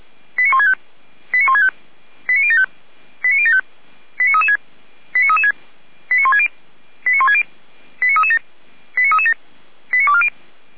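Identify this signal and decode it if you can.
Five-tone selective call (ZVEI), heard as quick runs of five tones, tones of about 70 ms. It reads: 89165 89165 89075 89075 89336 89336 89160 89160 89337 89337 89240
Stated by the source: ZVEI five-tone selective call received on SDR.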